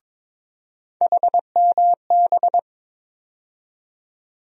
Morse code, heard as a keyed tone, about 22 wpm, 700 Hz. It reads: HMB
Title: USA Call Sign Postfixes - 3 Letters 22wpm